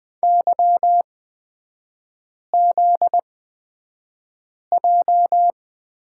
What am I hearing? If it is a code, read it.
Morse code, heard as a keyed tone, about 20 wpm, spaced as for 5 wpm, 700 Hz. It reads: YZJ
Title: 243 Lesson 39e - Letter z - Three Characters 20wpm